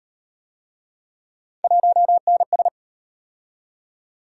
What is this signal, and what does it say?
Morse code, heard as a keyed tone, about 37 wpm, 700 Hz. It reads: 1NS